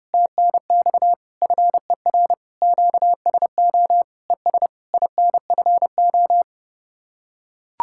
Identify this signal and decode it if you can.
Morse code, heard as a keyed tone, about 30 wpm, 700 Hz. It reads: TNX FER QSO ES INFO